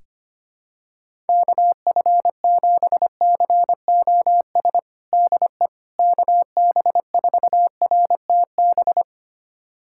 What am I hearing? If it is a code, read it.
Morse code, heard as a keyed tone, about 25 wpm, 700 Hz. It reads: KF7COS DE KB4RTB
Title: USA Callsign DE Callsign - 6 Characters - 25wpm